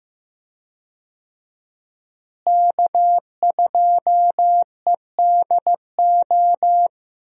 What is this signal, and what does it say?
Morse code, heard as a keyed tone, about 15 wpm, 700 Hz. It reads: K2EDO
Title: USA Call Signs - 5 Characters - Repeated 3x - 15wpm